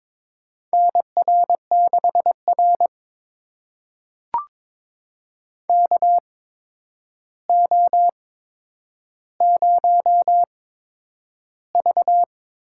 Morse code, heard as a keyed tone, about 22 wpm, spaced as for 9 wpm, 700 Hz. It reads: NR6R KO0V
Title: USA Call Signs - 4 Characters - 8x Character Spacing - 22wpm